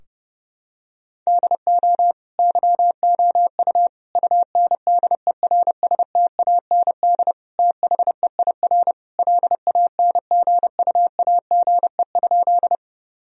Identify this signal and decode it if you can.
Morse code, heard as a keyed tone, about 32 wpm, 700 Hz. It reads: DO YOU UNDERSTAND THEIR LANGUAGE?
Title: Sentences from Top 1000 Words - Review Part 3 - 30wpm